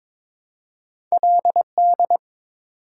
Morse code, heard as a keyed tone, about 22 wpm, 690 Hz. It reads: LD
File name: practice / 22wpm / Two Letter-Number Combinations - 22wpm